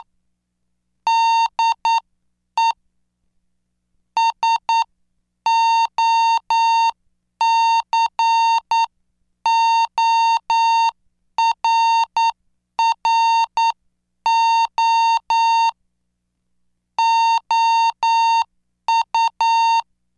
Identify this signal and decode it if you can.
Morse code, heard as a keyed tone, about 9 wpm, spaced as for 8 wpm, 910 Hz. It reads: DE SOCORRO OU